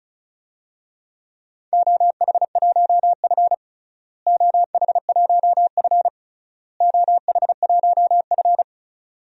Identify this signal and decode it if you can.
Morse code, heard as a keyed tone, about 36 wpm, 700 Hz. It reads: OH1F OH1F OH1F
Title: DXCC-100 Call Signs - 4 Characters - Repeated 3x - 35wpm